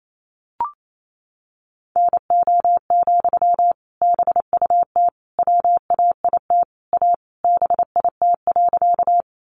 Morse code, heard as a keyed tone, about 28 wpm, 700 Hz. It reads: NO, BUT WAIT A BIT.
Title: Sentences from Top 900 Words 28wpm